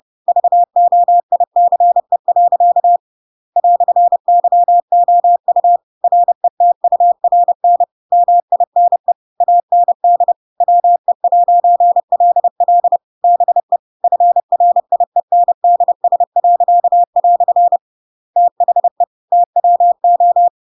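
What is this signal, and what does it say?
Morse code, heard as a keyed tone, about 30 wpm, 700 Hz. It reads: VOICE. "YOU RETURN MINE AND WE'LL BE FRIENDS." THE TWO
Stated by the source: Chapter 2 of The Adventures of Pinocchio read as a book in Morse code (CW) at 30 wpm, by ik1hge.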